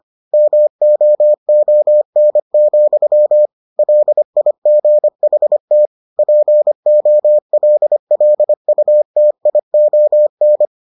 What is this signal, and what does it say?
Morse code, heard as a keyed tone, about 25 wpm, 600 Hz. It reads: MOON, LIGHT POLLUTION